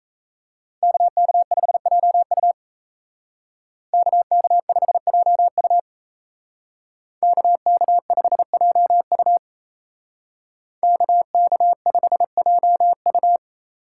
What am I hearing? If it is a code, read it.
Morse code, heard as a keyed tone, about 33 wpm, 700 Hz. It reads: KK5JU KK5JU KK5JU KK5JU
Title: Speed Racing - USA 2x2 Call Signs 28wpm